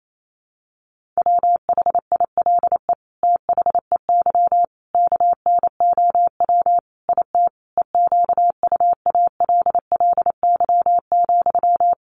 Morse code, heard as a keyed tone, about 28 wpm, 700 Hz. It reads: WHILE THEY KNOW IT EQUALLY,
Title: Sentences from Top 1400 Words 28wpm